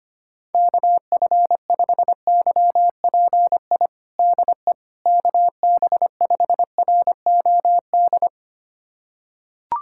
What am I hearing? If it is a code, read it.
Morse code, heard as a keyed tone, about 25 wpm, 700 Hz. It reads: KF5YPI DE KB5ROD
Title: USA Callsign DE Callsign 25wpm